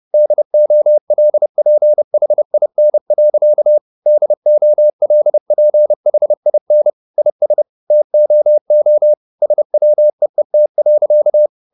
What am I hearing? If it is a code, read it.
Morse code, heard as a keyed tone, about 30 wpm, 600 Hz. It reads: DOLPHIN. DOLPHIN IS TOO SWEET.
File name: the old man and the sea / the old man and the sea 1 30wpm